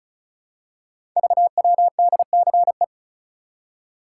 Morse code, heard as a keyed tone, about 36 wpm, 700 Hz. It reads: VWDCE
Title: Code Groups - 5 Letters - 35wpm